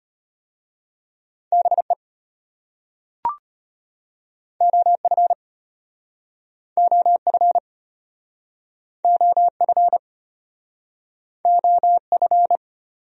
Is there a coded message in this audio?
Yes: Morse code from 1.5 to 12.6 s, heard as a keyed tone, about 33 wpm, 700 Hz. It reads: BE OF OF OF OF